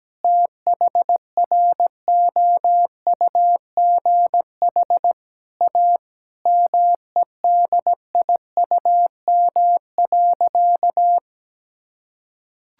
Morse code, heard as a keyed tone, about 17 wpm, 700 Hz. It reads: THROUGH A MEDIUM.
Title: Sentences from Top 1000 Words - Review Part 6 - 17wpm